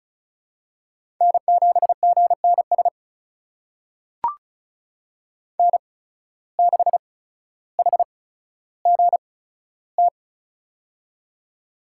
Morse code, heard as a keyed tone, about 34 wpm, spaced as for 15 wpm, 700 Hz. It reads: N7GNS N6HGT
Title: USA Call Signs - 5 Characters - 8x Character Spacing - 35wpm